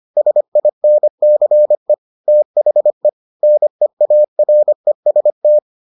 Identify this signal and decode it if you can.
Morse code, heard as a keyed tone, about 25 wpm, 600 Hz. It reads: SINCE THE NEAREST